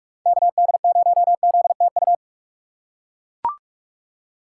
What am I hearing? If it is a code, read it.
Morse code, heard as a keyed tone, about 41 wpm, 700 Hz. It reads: KD0ZTU
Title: USA Call Signs - 6 Characters - 2x Character Spacing - 45wpm